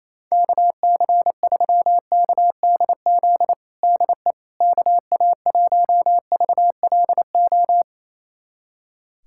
Morse code, heard as a keyed tone, about 28 wpm, 700 Hz. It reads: KC3KDZ DE KA1VLO
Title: USA Callsign DE Callsign - 6 Characters - 28wpm